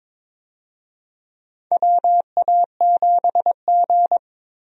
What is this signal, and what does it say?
Morse code, heard as a keyed tone, about 22 wpm, 700 Hz. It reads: WA7G
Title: USA Call Signs - 4 Characters - Repeated 2x - 22wpm